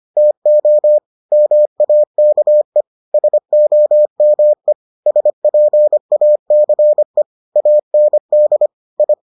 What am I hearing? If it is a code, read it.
Morse code, heard as a keyed tone, about 25 wpm, 600 Hz. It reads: TO MAKE SOME SPACE AND I